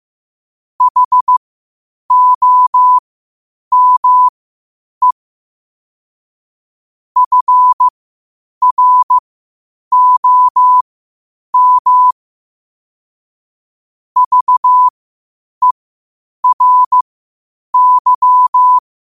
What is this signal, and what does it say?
Morse code, heard as a keyed tone, about 15 wpm, spaced as for 8 wpm, 1000 Hz. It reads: HOME FROM VERY
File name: 8wpm